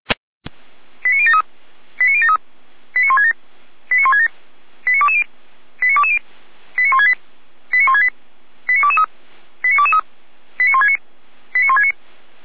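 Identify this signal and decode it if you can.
Five-tone selective call (ZVEI), heard as quick runs of five tones, tones of about 70 ms. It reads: 89073 89073 89167 89167 89220 89220 89168 89168 89223 89223 89169 89169
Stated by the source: ZVEI five-tone selective call received on an SDR, from the Feuerwehr Orlamünde audio archive.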